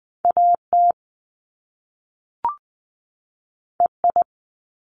Morse code, heard as a keyed tone, about 20 wpm, 700 Hz. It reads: AT EI
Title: Top Two Letter Combinations in Words 20wpm